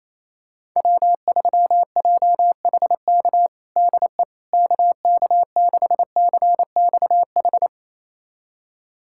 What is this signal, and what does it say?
Morse code, heard as a keyed tone, about 28 wpm, 700 Hz. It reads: W3JHK DE KK6CXH
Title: USA Callsign DE Callsign 28wpm